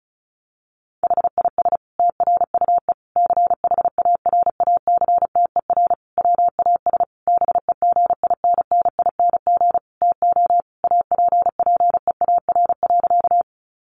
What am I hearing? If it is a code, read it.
Morse code, heard as a keyed tone, about 34 wpm, 700 Hz. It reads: HIS TRUE CHARACTER WAS BEGINNING TO APPEAR.
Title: Sentences from Top 1200 Words 35wpm